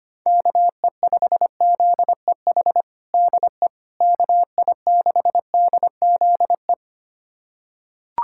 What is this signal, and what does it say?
Morse code, heard as a keyed tone, about 25 wpm, 700 Hz. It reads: KE5ZEH DE KI6DZE
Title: USA Callsign DE Callsign 25wpm